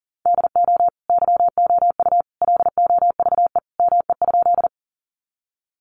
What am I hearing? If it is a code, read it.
Morse code, heard as a keyed tone, about 40 wpm, 700 Hz. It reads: DO YOU LOVE ME?
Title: Sentences from Top 400 Words 40wpm